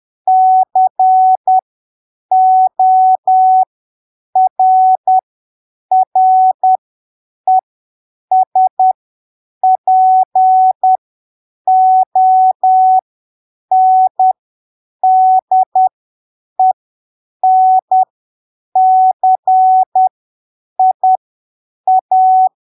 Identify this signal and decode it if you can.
Morse code, heard as a keyed tone, about 10 wpm, spaced as for 7 wpm, 750 Hz. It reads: CORRESPONDENCIA
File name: arquivos / texto / texto 28 10ppm